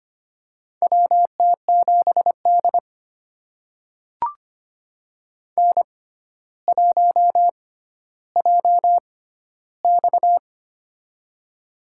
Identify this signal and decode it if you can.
Morse code, heard as a keyed tone, about 25 wpm, spaced as for 13 wpm, 700 Hz. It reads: WT7D N1JX
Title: USA Call Signs - 4 Characters - 6x Character Spacing - 25wpm